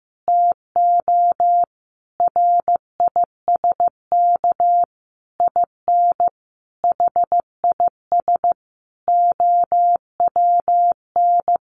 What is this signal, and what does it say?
Morse code, heard as a keyed tone, about 15 wpm, 700 Hz. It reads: TO RISK IN HIS OWN
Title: Sentences from Top 600 Words 15wpm